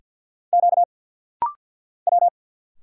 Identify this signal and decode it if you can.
Morse code, heard as a keyed tone, about 50 wpm, 700 Hz. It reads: Q W
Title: Single Letters - Rapid-Fire 50wpm